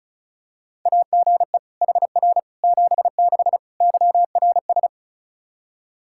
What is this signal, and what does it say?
Morse code, heard as a keyed tone, about 36 wpm, 700 Hz. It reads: AGE HR 76 YRS
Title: Practice QSO - Exchange 3 - More Info - 35wpm